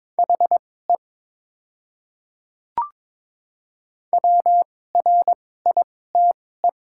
Morse code, heard as a keyed tone, about 22 wpm, spaced as for 16 wpm, 700 Hz. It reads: HE WRITE
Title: 2x Character Spacing - Top 200 Words - 22wpm